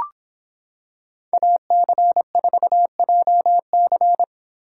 Morse code, heard as a keyed tone, about 26 wpm, 700 Hz. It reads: AC4JC